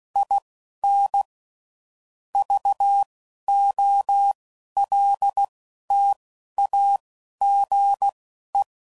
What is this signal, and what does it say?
Morse code, heard as a keyed tone, about 16 wpm, spaced as for 11 wpm, 800 Hz. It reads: IN VOLTAGE
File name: CW Code Groups @10WPM / Technical Subjects / XC